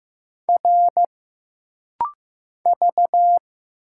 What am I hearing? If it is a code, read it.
Morse code, heard as a keyed tone, about 15 wpm, 700 Hz. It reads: R V
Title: Single Letters - Rapid-Fire 15wpm